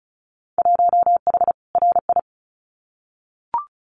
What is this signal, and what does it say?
Morse code, heard as a keyed tone, about 34 wpm, 700 Hz. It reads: 1H RI
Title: Winter Field Day Exchanges 35wpm